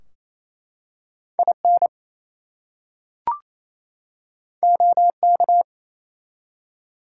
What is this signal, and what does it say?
Morse code, heard as a keyed tone, about 27 wpm, 700 Hz. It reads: IN OK
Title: Two Letter Words - 28wpm